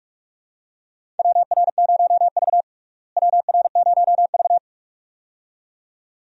Morse code, heard as a keyed tone, about 46 wpm, 700 Hz. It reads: WR0V WR0V
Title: USA Call Signs - 4 Characters - Repeated 2x - 45wpm